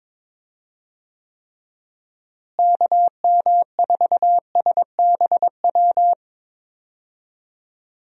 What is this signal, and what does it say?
Morse code, heard as a keyed tone, about 22 wpm, 700 Hz. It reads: KM4SBW
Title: USA Call Signs - 6 Characters - Repeated 3x - 22wpm